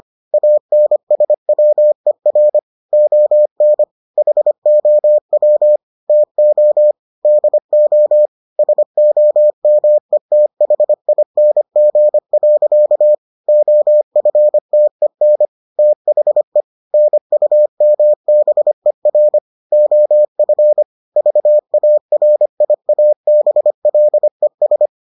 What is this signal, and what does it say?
Morse code, heard as a keyed tone, about 25 wpm, 600 Hz. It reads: ANSWER ON HOW TO DO SOMETHING. OFTEN THE NUMBER OF VARIABLES